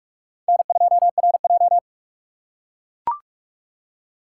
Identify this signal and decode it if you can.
Morse code, heard as a keyed tone, about 38 wpm, 700 Hz. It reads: N2RJ